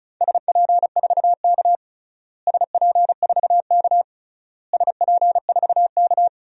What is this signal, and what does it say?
Morse code, heard as a keyed tone, about 34 wpm, 700 Hz. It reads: SP4K SP4K SP4K
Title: DXCC-100 Call Signs - 4 Characters - Repeated 3x - 35wpm